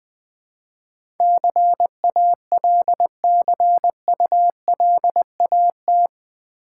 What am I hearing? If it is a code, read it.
Morse code, heard as a keyed tone, about 20 wpm, 700 Hz. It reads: CALCULAT
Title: Build A Word - 4 Letters At A Time - 11-Letter Words - 20wpm